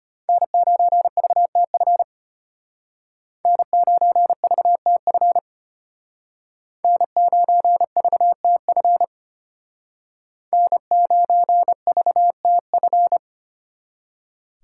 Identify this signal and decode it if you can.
Morse code, heard as a keyed tone, about 31 wpm, 700 Hz. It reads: N9VTF N9VTF N9VTF N9VTF
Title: Speed Racing - USA 1x3 Call Signs 25wpm